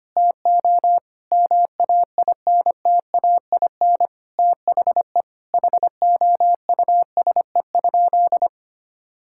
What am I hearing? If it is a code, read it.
Morse code, heard as a keyed tone, about 25 wpm, 700 Hz. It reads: TO MAINTAIN THE HOUSE?
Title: Sentences from Top 1000 Words - Review Part 5 - 25wpm